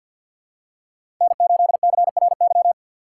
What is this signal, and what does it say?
Morse code, heard as a keyed tone, about 52 wpm, 700 Hz. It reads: N8CRY
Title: USA Call Signs - 5 Characters - 3x Character Spacing - 50wpm